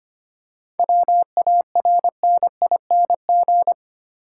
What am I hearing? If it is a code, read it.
Morse code, heard as a keyed tone, about 25 wpm, 700 Hz. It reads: WARNING
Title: Seven Letter Words - 25wpm